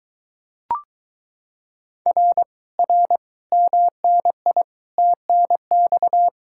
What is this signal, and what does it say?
Morse code, heard as a keyed tone, about 22 wpm, 700 Hz. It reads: R R MNI TNX